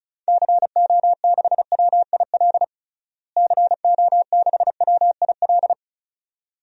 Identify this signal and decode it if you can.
Morse code, heard as a keyed tone, about 36 wpm, 700 Hz. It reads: CO6WIL CO6WIL
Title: DXCC-100 Call Signs - 6 Characters - Repeated 2x - 35wpm